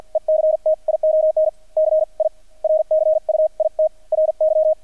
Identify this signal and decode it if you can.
Morse code, heard as a keyed tone, about 50 wpm, spaced as for 37 wpm, 630 Hz, with radio noise. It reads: ECTION CIRCUITRY
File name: archived / CRSnet